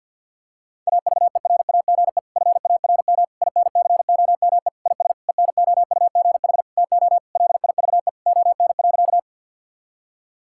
Morse code, heard as a keyed tone, about 47 wpm, 700 Hz. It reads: AVERAGE FARM INCOME IS ENOUGH TO LIVE ON.
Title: Sentences from Top 1000 Words - Review Part 6 - 50wpm